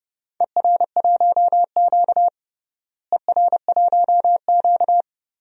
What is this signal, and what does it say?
Morse code, heard as a keyed tone, about 30 wpm, 700 Hz. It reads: ER1Q ER1Q
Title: DXCC-100 Call Signs - 4 Characters - Repeated 3x - 30wpm